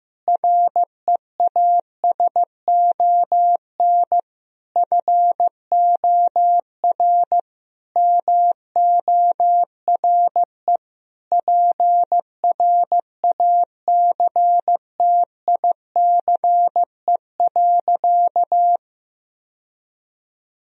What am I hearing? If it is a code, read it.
Morse code, heard as a keyed tone, about 15 wpm, 700 Hz. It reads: REASON FOR MORE PRACTICE.
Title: Sentences from Top 1000 Words - Review Part 4 - 15wpm